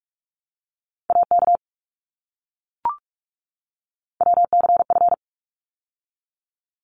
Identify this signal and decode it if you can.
Morse code, heard as a keyed tone, about 44 wpm, 700 Hz. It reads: AK WCF
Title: ARRL Sections 45wpm